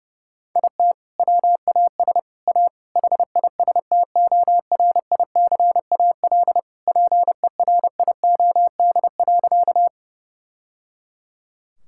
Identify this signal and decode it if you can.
Morse code, heard as a keyed tone, about 30 wpm, 700 Hz. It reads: IT WAS A HISTORICAL PERIOD.
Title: Sentences from Top 1000 Words - Review Part 5 - 30wpm